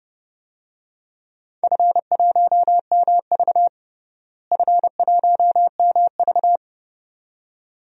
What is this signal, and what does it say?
Morse code, heard as a keyed tone, about 30 wpm, 700 Hz. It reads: F1MV F1MV
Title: DXCC-100 Call Signs - 4 Characters - Repeated 2x - 30wpm